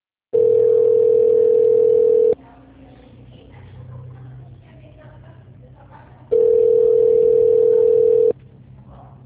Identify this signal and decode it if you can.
Ringback tone (North America), 440 + 480 Hz.